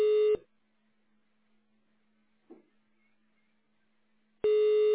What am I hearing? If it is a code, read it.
Ringback tone (Europe, 425 Hz).